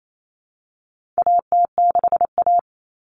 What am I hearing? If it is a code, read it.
Morse code, heard as a keyed tone, about 28 wpm, 700 Hz. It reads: AT6A
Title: Common Cut Numbers - 4-Digit 28wpm